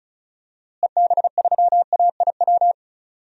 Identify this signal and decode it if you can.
Morse code, heard as a keyed tone, about 34 wpm, 700 Hz. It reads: EB3AIW
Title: DXCC-100 Call Signs - 6 Characters - Repeated 3x - 35wpm